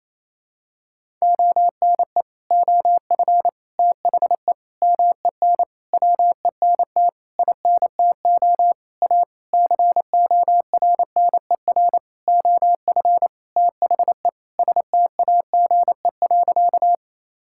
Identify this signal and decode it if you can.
Morse code, heard as a keyed tone, about 28 wpm, 700 Hz. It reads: ONE OF THE MEN WENT INTO A CORNER OF THE STAGE.
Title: Sentences from Top 1600 Words 28wpm